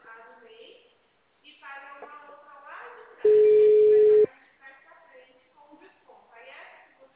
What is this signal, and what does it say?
Ringback tone (Europe, 425 Hz), with speech.